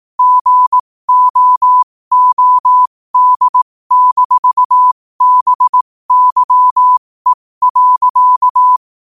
Morse code, heard as a keyed tone, about 18 wpm, spaced as for 15 wpm, 1000 Hz. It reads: GOOD-BYE.
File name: qotdfiles / 15wpm